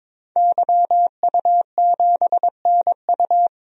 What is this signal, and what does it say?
Morse code, heard as a keyed tone, about 21 wpm, 700 Hz. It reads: YU7NU
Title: DXCC-100 Call Signs - 5 Characters - 22wpm